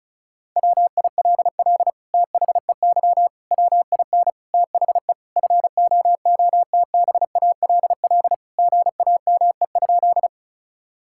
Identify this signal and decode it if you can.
Morse code, heard as a keyed tone, about 36 wpm, 700 Hz. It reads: WILL THEY WIN THE FOOTBALL GAME?
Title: Sentences from Top 1600 Words 35wpm